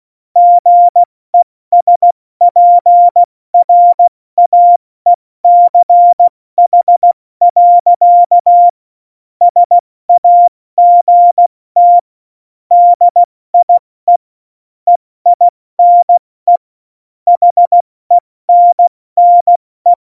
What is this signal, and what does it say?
Morse code, heard as a keyed tone, about 16 wpm, 700 Hz. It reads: GESPRAECH. SAGT DIE EINE HENNE
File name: Morsekurs / Witze / Witz 12 Mai, 2020 70 F80